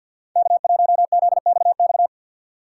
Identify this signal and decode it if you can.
Morse code, heard as a keyed tone, about 51 wpm, 690 Hz. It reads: K1ZXX
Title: USA Call Signs - 5 Characters - 3x Character Spacing - 50wpm